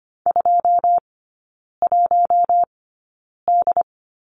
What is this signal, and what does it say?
Morse code, heard as a keyed tone, about 25 wpm, spaced as for 9 wpm, 700 Hz. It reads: 21D
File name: Lesson 15e - Number 5 - Three Characters 25wpm